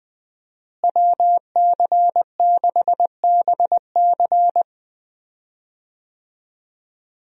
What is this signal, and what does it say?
Morse code, heard as a keyed tone, about 20 wpm, 700 Hz. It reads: WC6BC
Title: USA Call Signs - 5 Characters - Repeated 3x - 20wpm